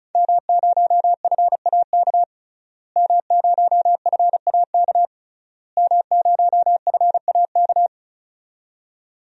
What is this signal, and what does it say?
Morse code, heard as a keyed tone, about 36 wpm, 700 Hz. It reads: M0FAK M0FAK M0FAK
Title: DXCC-100 Call Signs - 5 Characters - Repeated 3x - 35wpm